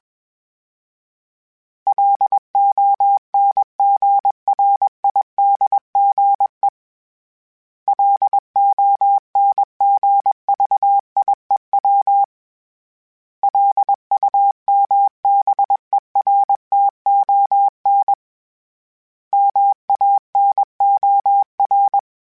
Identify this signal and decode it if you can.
Morse code, heard as a keyed tone, about 20 wpm, 800 Hz. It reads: LONGRIDGE LONGVIEW LUMBERTON MANOR
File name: fromSMs / FromSMs mp3s / Cities@20